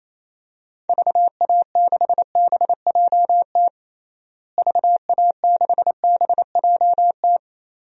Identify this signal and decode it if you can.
Morse code, heard as a keyed tone, about 28 wpm, 700 Hz. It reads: VA6BJT VA6BJT
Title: DXCC-100 Call Signs - 6 Characters - Repeated 2x - 28wpm